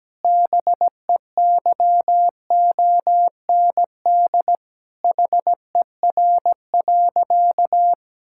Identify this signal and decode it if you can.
Morse code, heard as a keyed tone, about 17 wpm, 700 Hz. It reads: BEYOND HER.
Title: Sentences from Top 1000 Words - Review Part 6 - 17wpm